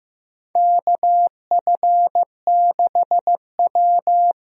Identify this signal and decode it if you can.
Morse code, heard as a keyed tone, about 15 wpm, 700 Hz. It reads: KF6W
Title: USA Call Signs - 4 Characters - Repeated 2x - 15wpm